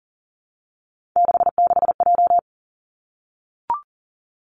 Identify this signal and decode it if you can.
Morse code, heard as a keyed tone, about 40 wpm, 700 Hz. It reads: 66J
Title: Three Challenging Letter-Number Combinations 40wpm